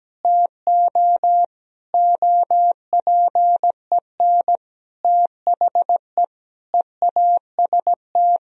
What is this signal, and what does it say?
Morse code, heard as a keyed tone, about 17 wpm, 700 Hz. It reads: TO OPEN THE EAST
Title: Sentences from Top 1000 Words - Review Part 3 - 17wpm